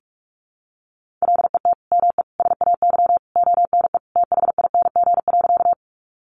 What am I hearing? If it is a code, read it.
Morse code, heard as a keyed tone, about 44 wpm, 700 Hz. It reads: LET ME SAY ONE THING.
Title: Sentences from Top 300 Words 45wpm